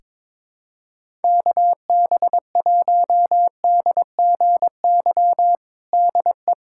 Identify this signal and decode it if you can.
Morse code, heard as a keyed tone, about 22 wpm, 700 Hz. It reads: KB1DGY DE
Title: USA Callsign DE Callsign - 6 Characters - 22wpm